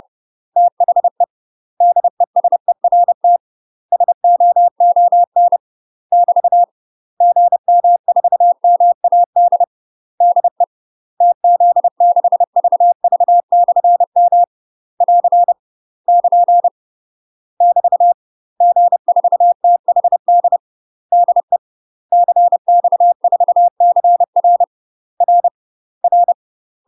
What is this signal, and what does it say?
Morse code, heard as a keyed tone, about 30 wpm, 700 Hz. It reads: THE DESERT SOON <BT> GM4MAD DE TZ6VV/M <AR> <KN> <BT> G4THD DE CX4CR R R
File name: UK MASTER QSOs TRAINING / 30WPM / UK MASTER QSOs 0022